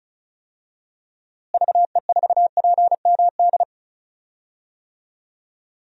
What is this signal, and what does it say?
Morse code, heard as a keyed tone, about 34 wpm, 700 Hz. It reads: VE4PMD
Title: DXCC-100 Call Signs - 6 Characters - Repeated 3x - 35wpm